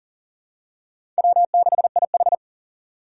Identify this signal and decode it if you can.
Morse code, heard as a keyed tone, about 40 wpm, 700 Hz. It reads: W6IH
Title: USA Call Signs - 4 Characters - 6x Character Spacing - 40wpm